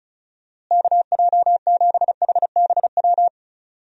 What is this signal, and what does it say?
Morse code, heard as a keyed tone, about 36 wpm, 700 Hz. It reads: KJ7HBW
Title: USA Call Signs - 6 Characters - Repeated 3x - 35wpm